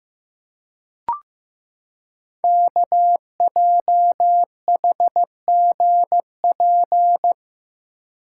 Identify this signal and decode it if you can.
Morse code, heard as a keyed tone, about 15 wpm, 700 Hz. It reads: KJHGP
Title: Code Groups - 5 Letters - 15wpm